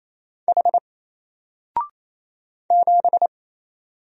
Morse code, heard as a keyed tone, about 28 wpm, 700 Hz. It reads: H 7
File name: Single-Letter-Number-28wpm